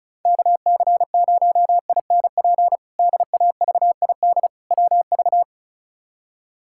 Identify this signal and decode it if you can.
Morse code, heard as a keyed tone, about 36 wpm, 700 Hz. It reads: KC0INP DAVID WV
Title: K1USN SST QSO Practice - 35wpm